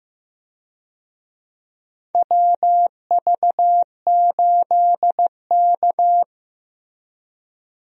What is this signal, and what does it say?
Morse code, heard as a keyed tone, about 15 wpm, 700 Hz. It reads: WV8K
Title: USA Call Signs - 4 Characters - Repeated 3x - 15wpm